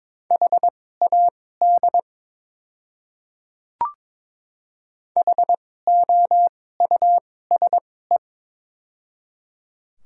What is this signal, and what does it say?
Morse code, heard as a keyed tone, about 22 wpm, spaced as for 16 wpm, 700 Hz. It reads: HAD HOUSE
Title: 2x Character Spacing - Top 200 Words - 22wpm